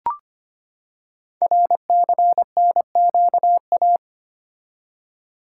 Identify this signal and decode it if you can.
Morse code, heard as a keyed tone, about 25 wpm, 700 Hz. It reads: RCNQA
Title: Code Groups - 5 Letters - 25wpm